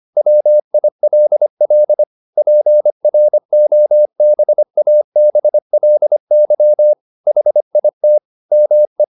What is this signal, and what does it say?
Morse code, heard as a keyed tone, about 25 wpm, 600 Hz. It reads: WILL PROBABLY HIT ME